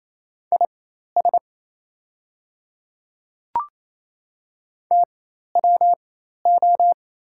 Morse code, heard as a keyed tone, about 28 wpm, spaced as for 13 wpm, 700 Hz. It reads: IS TWO
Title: 4x Character Spacing - Top 100 Words - 28wpm